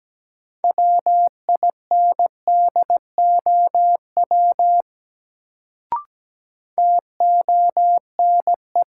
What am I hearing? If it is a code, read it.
Morse code, heard as a keyed tone, about 17 wpm, 700 Hz. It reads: WINDOW TONE
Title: Sentences from Top 1600 Words - New Word Drill Alt - 17wpm